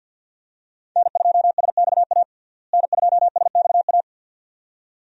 Morse code, heard as a keyed tone, about 53 wpm, 700 Hz. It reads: N2SXA N2SXA
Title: USA Call Signs - 5 Characters - Repeated 2x - 50wpm